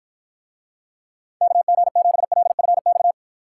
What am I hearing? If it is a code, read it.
Morse code, heard as a keyed tone, about 53 wpm, 700 Hz. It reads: KG7LFX